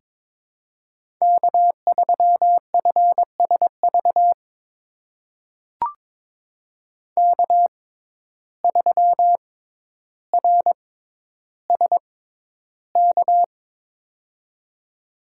Morse code, heard as a keyed tone, about 22 wpm, spaced as for 11 wpm, 700 Hz. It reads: K3FSV K3RSK
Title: USA Call Signs - 5 Characters - 6x Character Spacing - 22wpm